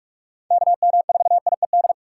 Morse code, heard as a keyed tone, about 44 wpm, 700 Hz. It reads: KM4IED
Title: USA Call Signs - 4x Character Spacing - 45wpm